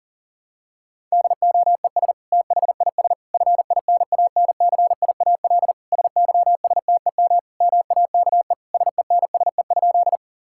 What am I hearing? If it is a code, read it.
Morse code, heard as a keyed tone, about 40 wpm, 700 Hz. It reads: DOES THIS FINANCIAL SYSTEM MAKE SENSE?
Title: Sentences from Top 1000 Words - Review Part 5 - 40wpm